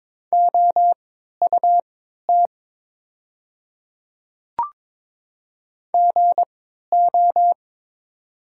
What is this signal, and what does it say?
Morse code, heard as a keyed tone, about 22 wpm, spaced as for 12 wpm, 690 Hz. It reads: OUT GO
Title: 3x Character Spacing - Top 100 Words - 22wpm